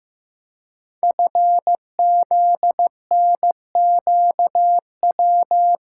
Morse code, heard as a keyed tone, about 15 wpm, 700 Hz. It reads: FZNQW
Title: Code Groups - 5 Letters - 15wpm